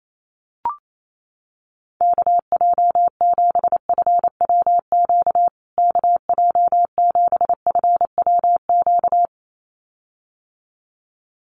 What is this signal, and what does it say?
Morse code, heard as a keyed tone, about 28 wpm, 700 Hz. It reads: KJ7FWQ KJ7FWQ
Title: USA Call Signs Repeated 2x 28wpm